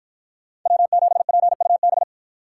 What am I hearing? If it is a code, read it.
Morse code, heard as a keyed tone, about 53 wpm, 700 Hz. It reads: W7PUB